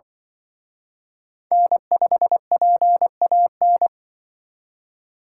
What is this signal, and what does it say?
Morse code, heard as a keyed tone, about 25 wpm, 700 Hz. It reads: N5PAN